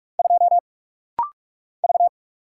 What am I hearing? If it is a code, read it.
Morse code, heard as a keyed tone, about 44 wpm, 700 Hz. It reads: 2 V
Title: Single Letter-Number Alt - Rapid-Fire 45wpm